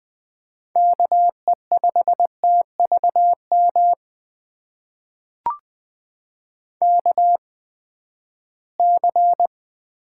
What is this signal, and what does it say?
Morse code, heard as a keyed tone, about 20 wpm, spaced as for 14 wpm, 700 Hz. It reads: KE5TVM KC
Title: USA Call Signs - 6 Characters - 8x Character Spacing - 20wpm